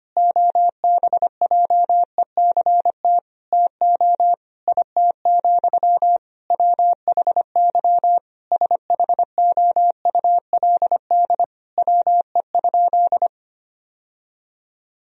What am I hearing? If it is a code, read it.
Morse code, heard as a keyed tone, about 25 wpm, 700 Hz. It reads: OBJECT TO IT, WHY SHOULD WE?
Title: Sentences from Top 1000 Words - Review Part 3 - 25wpm